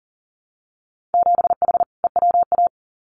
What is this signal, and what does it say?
Morse code, heard as a keyed tone, about 40 wpm, 700 Hz. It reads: 7H EWA
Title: Winter Field Day Exchanges Encore 40wpm